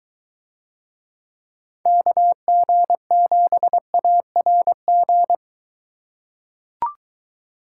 Morse code, heard as a keyed tone, about 22 wpm, 700 Hz. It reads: KG7ARG